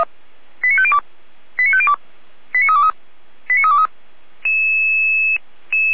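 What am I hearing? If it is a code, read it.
Five-tone selective call (ZVEI), heard as quick runs of five tones, tones of about 70 ms. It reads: 89502 89502 89323 89323
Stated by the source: Five-tone ZVEI selective call received by SDR.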